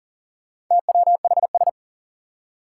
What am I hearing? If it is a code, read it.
Morse code, heard as a keyed tone, about 40 wpm, 700 Hz. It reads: TWHS
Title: Four Challenging Letter-Number Combinations 40wpm